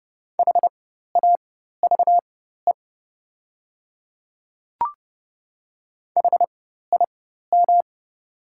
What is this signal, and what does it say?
Morse code, heard as a keyed tone, about 30 wpm, spaced as for 14 wpm, 700 Hz. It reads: HAVE HIM
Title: 4x Character Spacing - Top 100 Words - 30wpm